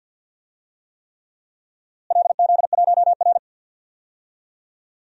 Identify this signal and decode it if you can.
Morse code, heard as a keyed tone, about 49 wpm, 700 Hz. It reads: LZ1R
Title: DXCC-100 Call Signs - 4 Characters - Repeated 3x - 50wpm